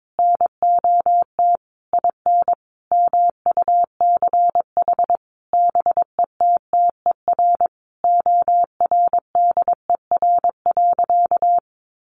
Morse code, heard as a keyed tone, about 22 wpm, 700 Hz. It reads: NOT IN MUCH BETTER ORDER.
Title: Sentences from Top 1500 Words 22wpm